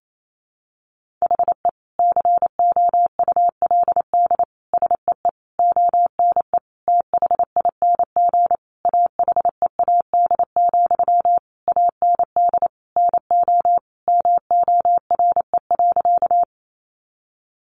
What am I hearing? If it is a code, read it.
Morse code, heard as a keyed tone, about 28 wpm, 700 Hz. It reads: HE COULD SEE ONE THING AHEAD, AND NO MORE.
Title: Sentences from Top 1100 Words 28wpm